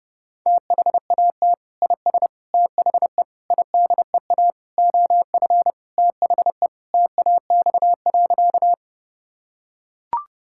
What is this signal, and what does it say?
Morse code, heard as a keyed tone, about 32 wpm, 700 Hz. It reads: THAT IS THE IDEA OF THE TAX.
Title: Sentences from Top 1000 Words - Review Part 5 - 30wpm